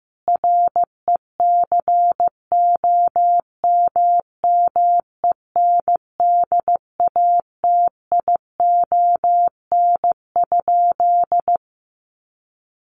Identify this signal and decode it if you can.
Morse code, heard as a keyed tone, about 14 wpm, 700 Hz. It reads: RECOMMENDATION?
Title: Sentences from Top 1400 Words 15wpm